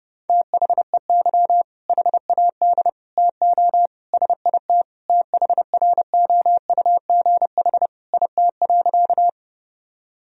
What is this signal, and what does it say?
Morse code, heard as a keyed tone, about 30 wpm, 700 Hz. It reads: THEY HAD TO SIT THROUGH IT.
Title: Sentences from Top 1000 Words - Review Part 1 - 30wpm